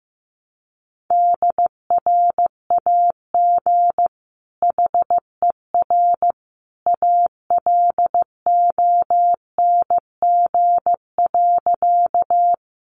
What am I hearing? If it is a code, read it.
Morse code, heard as a keyed tone, about 15 wpm, 700 Hz. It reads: DRAG HER ALONG.
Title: Sentences from Top 1500 Words 15wpm